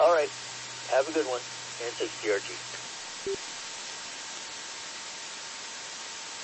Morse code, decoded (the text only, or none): none